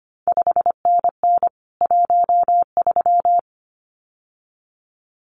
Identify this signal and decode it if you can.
Morse code, heard as a keyed tone, about 25 wpm, 700 Hz. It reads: 5NN 13
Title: CQ WW DX Contest Exchange - 25wpm